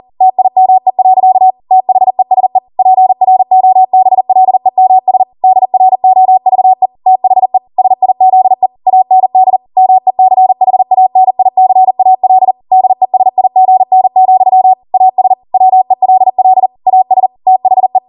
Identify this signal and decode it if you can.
Morse code, heard as a keyed tone, about 40 wpm, 750 Hz. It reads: TIME. THESE PROBLEMS DROVE THE SIZE AND MECHANICAL DESIGN, AS WELL AS THE